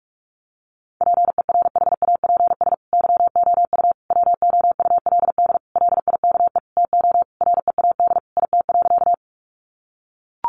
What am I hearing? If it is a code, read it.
Morse code, heard as a keyed tone, about 44 wpm, 700 Hz. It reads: PERHAPS YOU WOULD LIKE TO READ IT.
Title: Sentences from Top 500 Words 45wpm